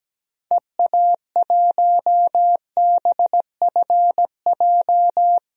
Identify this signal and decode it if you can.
Morse code, heard as a keyed tone, about 17 wpm, 700 Hz. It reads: EA1BFJ
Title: DXCC-100 Call Signs - 6 Characters - 17wpm